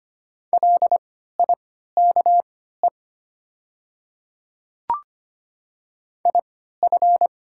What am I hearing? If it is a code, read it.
Morse code, heard as a keyed tone, about 25 wpm, spaced as for 14 wpm, 700 Hz. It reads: LIKE IF
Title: 3x Character Spacing - Top 100 Words - 25wpm